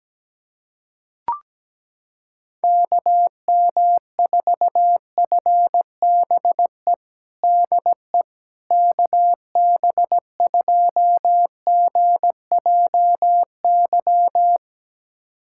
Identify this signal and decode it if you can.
Morse code, heard as a keyed tone, about 17 wpm, 700 Hz. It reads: KM4FBE DE KB2GJY
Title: USA Callsign DE Callsign - 6 Characters - 17wpm